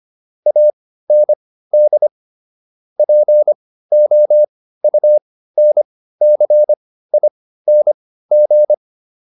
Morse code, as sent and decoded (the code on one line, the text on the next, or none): .- -. -.. / .--. --- ..- -. -.-. .. -. --.
AND POUNCING